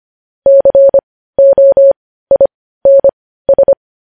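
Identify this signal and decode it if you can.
Morse code, heard as a keyed tone, about 25 wpm, spaced as for 15 wpm, 550 Hz. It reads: COINS
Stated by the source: Morse news headlines practice at 15.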